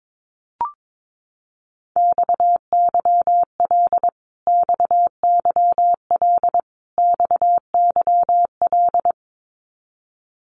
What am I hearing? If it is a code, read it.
Morse code, heard as a keyed tone, about 22 wpm, 700 Hz. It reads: XYL XYL XYL
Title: Abbreviations Repeated 3x - 22wpm